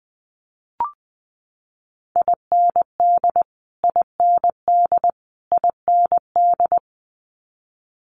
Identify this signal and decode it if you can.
Morse code, heard as a keyed tone, about 19 wpm, 700 Hz. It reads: IND IND IND
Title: Multi-letter Phonemes Repeated 3x 20wpm